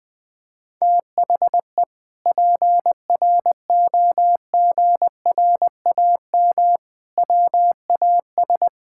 Morse code, heard as a keyed tone, about 20 wpm, 700 Hz. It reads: THE PROGRAM WAS
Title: Sentences from Top 1000 Words - Review Part 5 - 20wpm